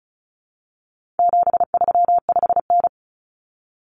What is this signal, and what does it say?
Morse code, heard as a keyed tone, about 34 wpm, 700 Hz. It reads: 735N
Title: Common Cut Numbers - 4-Digit 35wpm